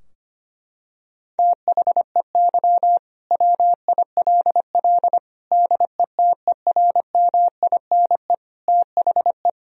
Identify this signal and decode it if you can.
Morse code, heard as a keyed tone, about 25 wpm, 700 Hz. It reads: THEY WILL DETERMINE THE
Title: Sentences from Top 1000 Words - Review Part 4 - 25wpm